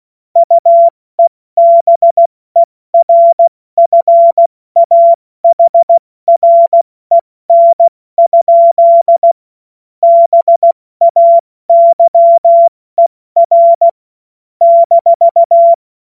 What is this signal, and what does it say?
Morse code, heard as a keyed tone, about 16 wpm, 680 Hz. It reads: UEBERFAHREN? BAYER -
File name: Witz-12-August-2025-70-F80